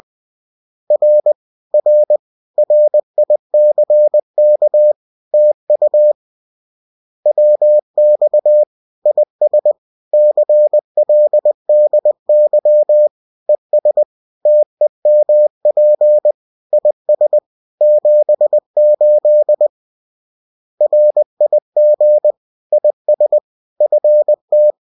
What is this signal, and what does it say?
Morse code, heard as a keyed tone, about 20 wpm, 600 Hz. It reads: R R RICK TU WX IS CLDY ES TEMP IS 78 RIG IS FT